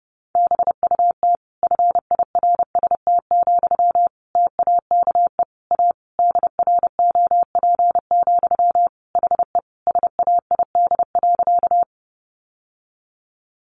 Morse code, heard as a keyed tone, about 30 wpm, 700 Hz. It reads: BUT FIRST, TAKE A DROP, HE SAID.
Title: Sentences from Top 1400 Words 30wpm